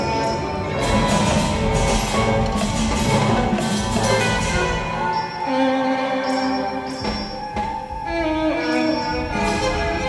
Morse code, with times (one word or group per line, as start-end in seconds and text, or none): none